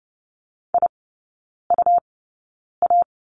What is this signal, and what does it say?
Morse code, heard as a keyed tone, about 30 wpm, spaced as for 9 wpm, 700 Hz. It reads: IUA
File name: Lesson 19e - Letter w - Three Characters 30wpm